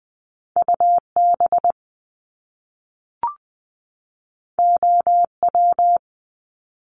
Morse code, heard as a keyed tone, about 20 wpm, 700 Hz. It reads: UB OW
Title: Top Two Letter Combinations in Words 20wpm